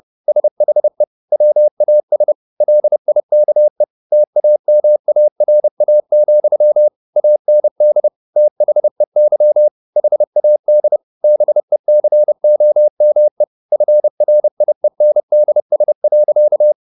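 Morse code, heard as a keyed tone, about 30 wpm, 600 Hz. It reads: SHE WAS LIKE TAMARA, AND THEY HAD BECOME FRIENDS.
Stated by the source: Morse code at 30 wpm, the novel Never by Ken Follett in English, from CW med Gnister.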